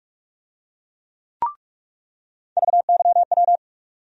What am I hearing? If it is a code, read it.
Morse code, heard as a keyed tone, about 43 wpm, 700 Hz. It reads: VYW
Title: USA Call Sign Postfixes - 3 Letters 45wpm